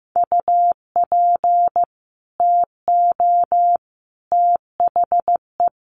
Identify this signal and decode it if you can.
Morse code, heard as a keyed tone, about 15 wpm, 700 Hz. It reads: UP TO THE
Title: Sentences from Top 800 Words 15wpm